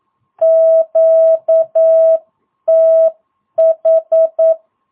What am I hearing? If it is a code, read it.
Morse code, heard as a keyed tone, about 9 wpm, 650 Hz. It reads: QTH
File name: jh8cbh / 2022cw